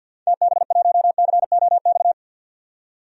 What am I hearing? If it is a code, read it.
Morse code, heard as a keyed tone, about 51 wpm, 700 Hz. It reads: TB1COX
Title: DXCC-100 Call Signs - 6 Characters - 50wpm